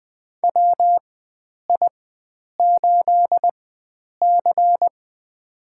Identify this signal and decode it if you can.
Morse code, heard as a keyed tone, about 20 wpm, spaced as for 9 wpm, 700 Hz. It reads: WI8C
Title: USA Call Signs - 4 Characters - 4x Character Spacing - 20wpm